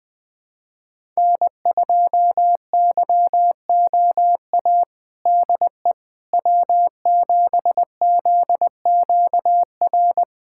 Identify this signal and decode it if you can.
Morse code, heard as a keyed tone, about 20 wpm, 700 Hz. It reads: N2YOA DE W7ZQR
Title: USA Callsign DE Callsign - 5 Characters - 20wpm